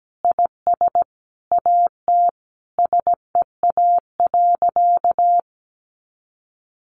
Morse code, heard as a keyed tone, about 17 wpm, 700 Hz. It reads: IS AT SEA.
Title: Sentences from Top 1400 Words 17wpm